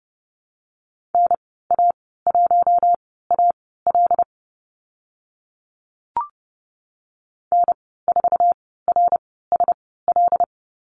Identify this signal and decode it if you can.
Morse code, heard as a keyed tone, about 30 wpm, spaced as for 17 wpm, 700 Hz. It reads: NA1AL N4RSL
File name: Lesson 14g - Number 2 - Call Signs 30wpm